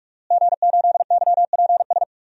Morse code, heard as a keyed tone, about 46 wpm, 700 Hz. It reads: G8YPS
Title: DXCC-100 Call Signs - 5 Characters - Repeated 2x - 45wpm